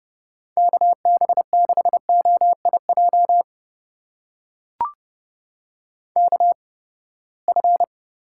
Morse code, heard as a keyed tone, about 31 wpm, spaced as for 21 wpm, 700 Hz. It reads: KB6OIJ KF
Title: USA Call Signs - 6 Characters - 8x Character Spacing - 30wpm